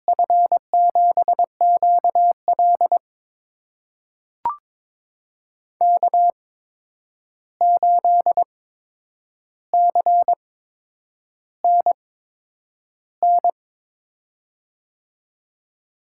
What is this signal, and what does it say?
Morse code, heard as a keyed tone, about 22 wpm, spaced as for 9 wpm, 700 Hz. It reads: F7QL K8CNN
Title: USA Call Signs - 5 Characters - 8x Character Spacing - 22wpm